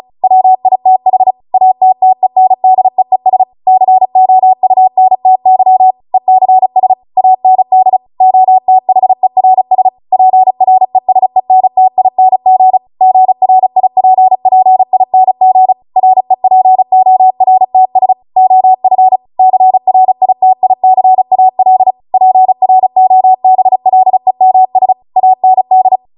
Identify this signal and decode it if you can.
Morse code, heard as a keyed tone, about 36 wpm, 750 Hz. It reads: WITH ATTENDEES COUNTY ECS AND OTHERS PRESENTING GRIPPING REPORTS OF CRITICAL PROBLEMS AND